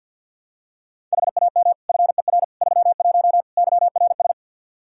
Morse code, heard as a keyed tone, about 46 wpm, 700 Hz. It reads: HAM FER 31 YRS
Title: Practice QSO - Exchange 3 - More Info - 50wpm